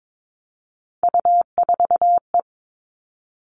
Morse code, heard as a keyed tone, about 22 wpm, 700 Hz. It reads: U4E
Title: All Cut Numbers - 3-Digit 22wpm